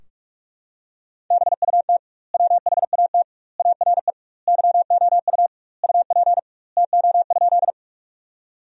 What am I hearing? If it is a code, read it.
Morse code, heard as a keyed tone, about 46 wpm, 700 Hz. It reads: BUT WHAT ARE YOU UP TO?
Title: 2x Word-Spacing - Sentences from Top 100 Words - 45wpm